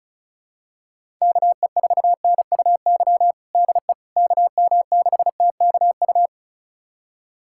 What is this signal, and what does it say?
Morse code, heard as a keyed tone, about 34 wpm, 700 Hz. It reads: KE4NUY DE KM6TKU
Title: USA Callsign DE Callsign 35wpm